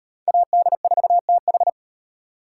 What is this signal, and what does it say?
Morse code, heard as a keyed tone, about 33 wpm, 700 Hz. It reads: AD4TH